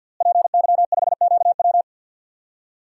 Morse code, heard as a keyed tone, about 47 wpm, 700 Hz. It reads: PY5QW
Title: DXCC-100 Call Signs - 5 Characters - Repeated 3x - 50wpm